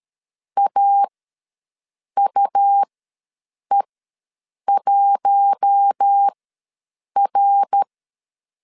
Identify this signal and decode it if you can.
Morse code, heard as a keyed tone, about 13 wpm, spaced as for 7 wpm, 800 Hz. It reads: AUE1R
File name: LEZIONI / 5 LBVC / 5 LBVC ws 12wpm 3159